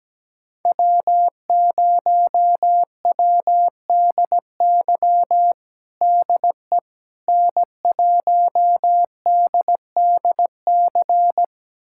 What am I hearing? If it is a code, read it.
Morse code, heard as a keyed tone, about 17 wpm, 700 Hz. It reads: W0WDY DE N1DDC
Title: USA Callsign DE Callsign - 5 Characters - 17wpm